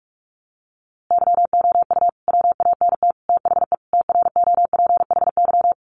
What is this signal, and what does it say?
Morse code, heard as a keyed tone, about 44 wpm, 700 Hz. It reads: YOU WANT THE TROPHY
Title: A Sound Of Thunder 45wpm